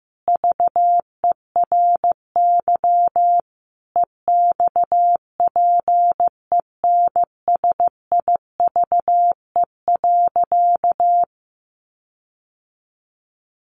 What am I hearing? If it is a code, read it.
Morse code, heard as a keyed tone, about 15 wpm, 700 Hz. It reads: VERY EXPENSIVE.
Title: Sentences from Top 1500 Words 15wpm